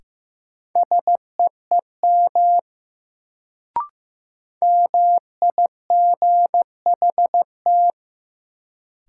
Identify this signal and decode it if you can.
Morse code, heard as a keyed tone, about 15 wpm, 700 Hz. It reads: SEEM MIGHT
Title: Sentences from Top 300 Words - New Word Drill - 15wpm